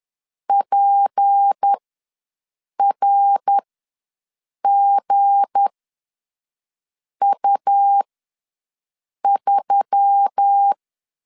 Morse code, heard as a keyed tone, about 10 wpm, spaced as for 5 wpm, 800 Hz. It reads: PRGU3
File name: LEZIONI / 7 GF9Z / 7 G ws 10wpm 3843